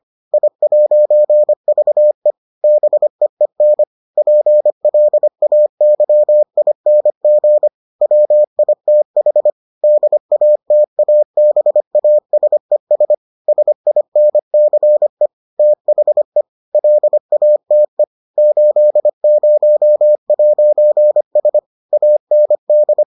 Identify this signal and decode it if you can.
Morse code, heard as a keyed tone, about 25 wpm, 600 Hz. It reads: I'VE BEEN PLAYING WITH DATABASES SINCE THE LATE 80'S AND